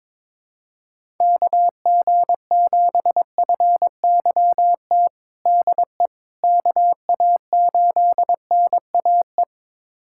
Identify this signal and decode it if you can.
Morse code, heard as a keyed tone, about 22 wpm, 700 Hz. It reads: KG7FYT DE KA8NAE
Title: USA Callsign DE Callsign - 6 Characters - 22wpm